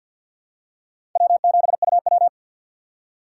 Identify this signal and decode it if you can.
Morse code, heard as a keyed tone, about 45 wpm, 700 Hz. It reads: W7UW
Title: USA Call Signs - 4 Characters - 2x Character Spacing - 50wpm